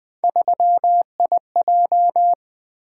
Morse code, heard as a keyed tone, about 20 wpm, 700 Hz. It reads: 3IJ
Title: Three Letter-Number Combinations - 20wpm